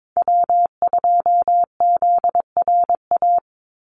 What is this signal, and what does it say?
Morse code, heard as a keyed tone, about 22 wpm, 700 Hz. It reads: W2ZRA
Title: Call Signs Repeated 2x - 22wpm